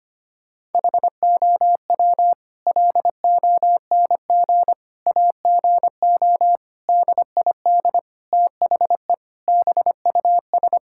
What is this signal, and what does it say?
Morse code, heard as a keyed tone, about 25 wpm, 700 Hz. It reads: HOW LONG AGO DID THE BUS
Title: Sentences from Top 1000 Words - Review Part 6 - 25wpm